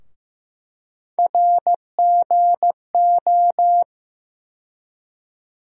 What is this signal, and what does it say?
Morse code, heard as a keyed tone, about 16 wpm, 700 Hz. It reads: RGO
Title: USA Call Sign Postfixes - 3 Letters 15wpm